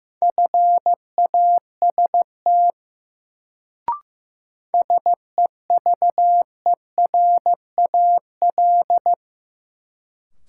Morse code, heard as a keyed tone, about 15 wpm, 700 Hz. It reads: FAST SEVERAL
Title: Sentences from Top 400 Words - New Word Drill - 15wpm